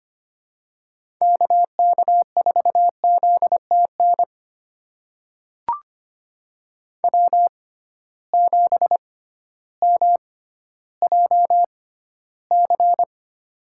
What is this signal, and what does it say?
Morse code, heard as a keyed tone, about 25 wpm, spaced as for 13 wpm, 700 Hz. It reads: KK4ZTN W7MJC
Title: USA Call Signs - 6x Character Spacing - 25wpm